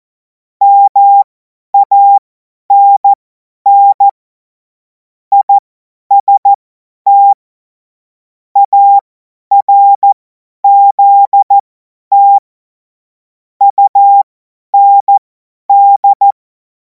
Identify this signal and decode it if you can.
Morse code, heard as a keyed tone, about 14 wpm, spaced as for 10 wpm, 800 Hz. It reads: MANN IST ARZT UND
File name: Morsekurs / Witze / Witz 04. August 2026 50 F70